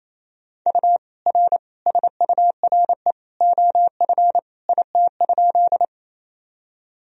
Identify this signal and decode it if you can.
Morse code, heard as a keyed tone, about 28 wpm, 700 Hz. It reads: U R SURE OF IT?
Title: Sentences from Top 300 Words and Abbreviations 28wpm